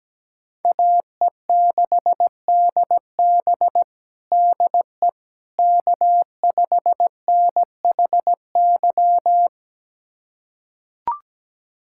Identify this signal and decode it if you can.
Morse code, heard as a keyed tone, about 17 wpm, 700 Hz. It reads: AE6DB DE K5NHY
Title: USA Callsign DE Callsign - 5 Characters - 17wpm